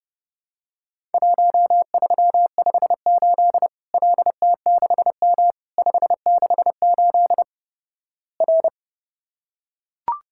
Morse code, heard as a keyed tone, about 32 wpm, 700 Hz. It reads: 1358 LT6M 568 R
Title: Worked All Europe DX Contest - QTC Exchange - 30wpm